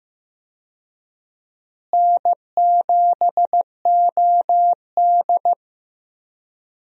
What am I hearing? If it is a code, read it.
Morse code, heard as a keyed tone, about 15 wpm, 700 Hz. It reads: N7OD
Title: USA Call Signs - 4 Characters - Repeated 3x - 15wpm